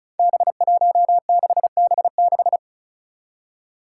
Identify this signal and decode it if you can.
Morse code, heard as a keyed tone, about 36 wpm, 700 Hz. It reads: B16B6
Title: Five Challenging Letter-Number Combinations 35wpm